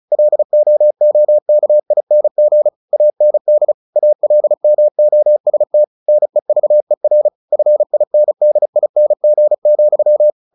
Morse code, heard as a keyed tone, about 34 wpm, 600 Hz. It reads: LOOKING AND ALMOST NEVER FINDING,
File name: the old man and the sea / the old man and the sea 1 35wpm